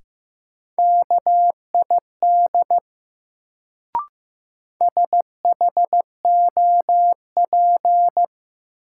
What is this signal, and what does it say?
Morse code, heard as a keyed tone, about 16 wpm, 700 Hz. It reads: KID SHOP
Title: Sentences from Top 1600 Words - New Word Drill - 15wpm